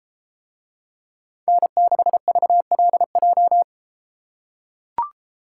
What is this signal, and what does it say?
Morse code, heard as a keyed tone, about 32 wpm, 700 Hz. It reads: N6VLJ